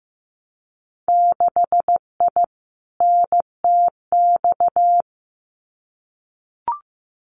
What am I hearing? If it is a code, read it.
Morse code, heard as a keyed tone, about 15 wpm, 700 Hz. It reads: 6I NTX
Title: Winter Field Day Exchanges 15wpm